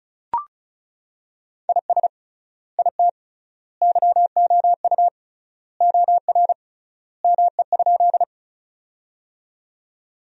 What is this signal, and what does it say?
Morse code, heard as a keyed tone, about 36 wpm, 700 Hz. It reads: IS IT YOU OR ME?
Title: 3x Word-Spacing - Sentences from Top 200 Words - 35wpm